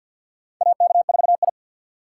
Morse code, heard as a keyed tone, about 49 wpm, 700 Hz. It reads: AK4I
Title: USA Call Signs - 3x Character Spacing - 50wpm